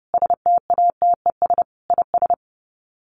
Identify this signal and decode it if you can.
Morse code, heard as a keyed tone, about 30 wpm, 700 Hz. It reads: STATES IS
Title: Sets of 5 Words Encore2 30wpm